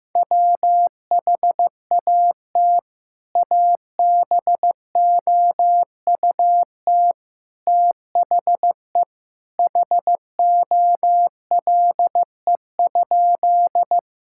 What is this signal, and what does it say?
Morse code, heard as a keyed tone, about 16 wpm, 700 Hz. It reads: WHAT ABOUT THE HOLE?
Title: Sentences from Top 1600 Words 15wpm